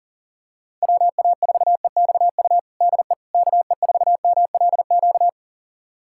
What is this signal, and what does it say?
Morse code, heard as a keyed tone, about 40 wpm, 700 Hz. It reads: WA4EXU DE KE4MLQ
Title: USA Callsign DE Callsign - 6 Characters - 40wpm